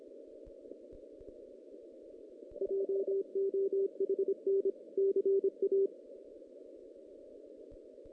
Morse code, heard as a keyed tone, about 27 wpm, 380 Hz. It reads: JOHN CA
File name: 2016 JAN CW NAQP